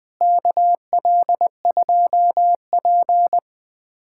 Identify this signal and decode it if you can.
Morse code, heard as a keyed tone, about 21 wpm, 700 Hz. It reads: KL2P
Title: USA Callsign DE Callsign - 4 Characters - 20wpm